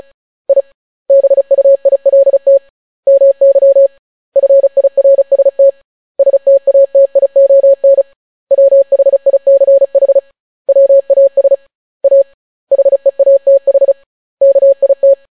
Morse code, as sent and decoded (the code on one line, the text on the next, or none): .. / -... ..- .. .-.. - / -- -.-- / ..-. .. .-. ... - / ... - .- - .. --- -. / .-- .... .. -.-. .... / .-- .- ... / .- / .... .- - .... / -.- .. -
I BUILT MY FIRST STATION WHICH WAS A HEATH KIT